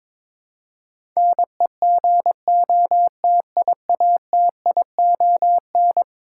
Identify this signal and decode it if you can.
Morse code, heard as a keyed tone, about 22 wpm, 700 Hz. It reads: NEGOTIATION
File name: practice / 22wpm / Eleven Letter Words - 22wpm